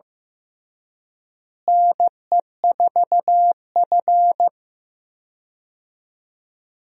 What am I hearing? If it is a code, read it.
Morse code, heard as a keyed tone, about 15 wpm, 700 Hz. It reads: NE4F
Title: USA Call Signs - 4 Characters - Repeated 2x - 15wpm